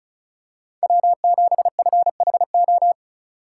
Morse code, heard as a keyed tone, about 36 wpm, 700 Hz. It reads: W7FHO
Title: USA Call Signs - 5 Characters - 4x Character Spacing - 35wpm